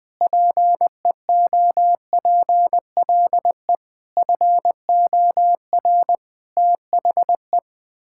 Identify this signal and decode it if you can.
Morse code, heard as a keyed tone, about 20 wpm, 700 Hz. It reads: PEOPLE FOR THE
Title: Sentences with 9-Word Phrases - Spoken Beginning - 20wpm